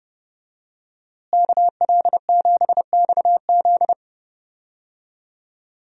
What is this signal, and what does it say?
Morse code, heard as a keyed tone, about 30 wpm, 700 Hz. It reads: KL7XZ
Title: DXCC-100 Call Signs - 5 Characters - Repeated 3x - 30wpm